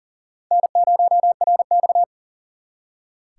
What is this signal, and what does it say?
Morse code, heard as a keyed tone, about 53 wpm, spaced as for 47 wpm, 700 Hz. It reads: N0RX
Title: Speed Racing - USA 1x2 Call Signs 40wpm